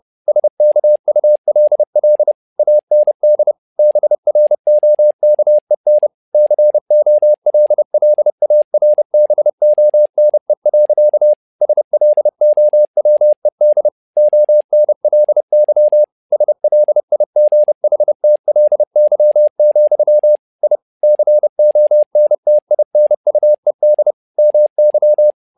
Morse code, heard as a keyed tone, about 30 wpm, 600 Hz. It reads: SKULL AND BROKEN COLLARBONE. SLOWED ONLY SLIGHTLY, I CONTINUED MY
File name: learning center / tower trouble / tower trouble 30wpm